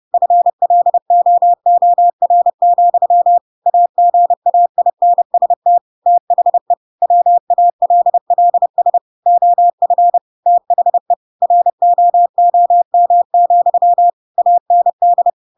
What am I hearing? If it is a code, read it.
Morse code, heard as a keyed tone, about 30 wpm, 700 Hz. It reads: FLOOR, AGAINST THE WALLS OF THE ROOM, AND
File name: Pinocchio / Pinocchio ch1 30 30 ik1hge